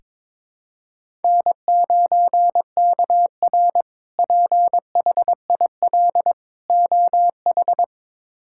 Morse code, heard as a keyed tone, about 22 wpm, 700 Hz. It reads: N9KR PHIL OH
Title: K1USN SST QSO Practice - 22wpm